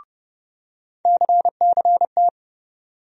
Morse code, heard as a keyed tone, about 30 wpm, 700 Hz. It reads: CCT
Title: USA Call Sign Postfixes - 3 Letters 30wpm